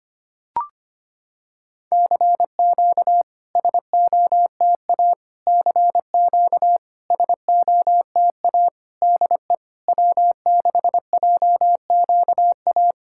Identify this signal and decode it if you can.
Morse code, heard as a keyed tone, about 25 wpm, 700 Hz. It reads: CQ SOTA CQ SOTA DE W6JQA
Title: SOTA QSO Practice 25wpm